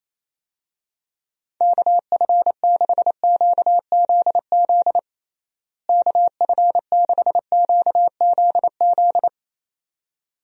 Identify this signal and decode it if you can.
Morse code, heard as a keyed tone, about 27 wpm, 700 Hz. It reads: KF6QZZ KF6QZZ
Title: USA Call Signs - 6 Characters - Repeated 2x - 28wpm